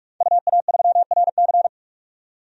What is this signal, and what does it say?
Morse code, heard as a keyed tone, about 44 wpm, 700 Hz. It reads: UA3RC
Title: DXCC-100 Call Signs - 5 Characters - 45wpm